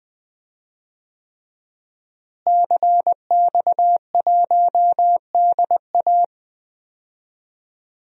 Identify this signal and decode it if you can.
Morse code, heard as a keyed tone, about 21 wpm, 700 Hz. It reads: CX1DA
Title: DXCC-100 Call Signs - 5 Characters - Repeated 3x - 20wpm